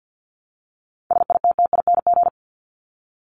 Morse code, heard as a keyed tone, about 50 wpm, 700 Hz. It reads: SITTING
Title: ING Suffix - Top 100 Words with Base Words 50wpm